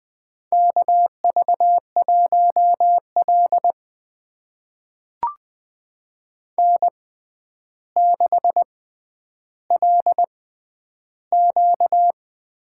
Morse code, heard as a keyed tone, about 20 wpm, spaced as for 10 wpm, 700 Hz. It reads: KV1L N6LQ
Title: USA Call Signs - 4 Characters - 6x Character Spacing - 20wpm